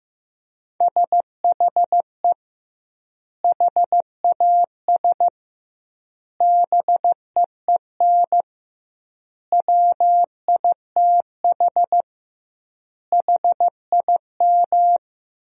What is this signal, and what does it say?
Morse code, heard as a keyed tone, about 16 wpm, 700 Hz. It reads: SHE HAS BEEN WITH HIM
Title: 2x Word-Spacing - Sentences from Top 200 Words - 15wpm